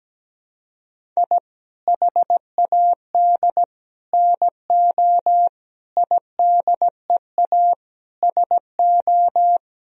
Morse code, heard as a keyed tone, about 17 wpm, 700 Hz. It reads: I HAD NO IDEA SO
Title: Sentences from Top 1000 Words - Review Part 4 - 17wpm